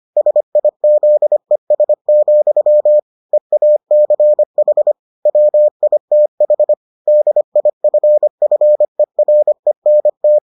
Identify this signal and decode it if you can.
Morse code, heard as a keyed tone, about 25 wpm, 600 Hz. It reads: SIZES, EACH WITH DIFFERENT